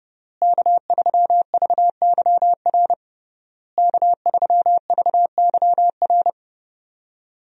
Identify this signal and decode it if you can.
Morse code, heard as a keyed tone, about 32 wpm, 700 Hz. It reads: K3VYR K3VYR
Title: USA Call Signs - 5 Characters - Repeated 3x - 30wpm